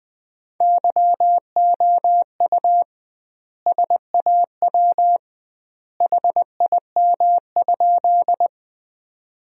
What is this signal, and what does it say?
Morse code, heard as a keyed tone, about 21 wpm, 700 Hz. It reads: YOU SAW HIM?
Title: 2x Word-Spacing - Sentences from Top 300 Words - 20wpm